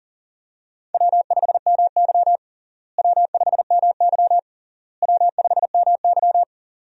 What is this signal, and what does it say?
Morse code, heard as a keyed tone, about 40 wpm, 700 Hz. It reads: W5MY W5MY W5MY
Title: USA Call Signs - 4 Characters - Repeated 3x - 40wpm